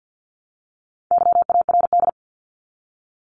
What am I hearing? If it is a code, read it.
Morse code, heard as a keyed tone, about 48 wpm, 700 Hz. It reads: YARD
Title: Top 1751-2000 Words 50wpm